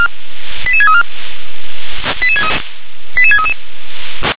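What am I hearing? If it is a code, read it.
Five-tone selective call (ZVEI), heard as quick runs of five tones, tones of about 70 ms. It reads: 88634 88633
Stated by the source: ZVEI five-tone selective call, code 88730, received with an SDR.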